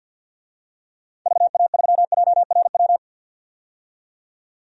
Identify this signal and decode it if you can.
Morse code, heard as a keyed tone, about 47 wpm, 700 Hz. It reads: VA3JRW
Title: DXCC-100 Call Signs - 6 Characters - 50wpm